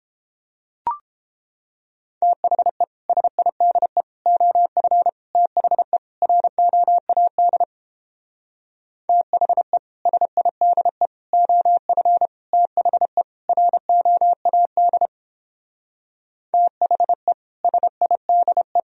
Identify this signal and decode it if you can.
Morse code, heard as a keyed tone, about 29 wpm, 700 Hz. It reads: THE SIDE OF THE ROAD THE SIDE OF THE ROAD THE SIDE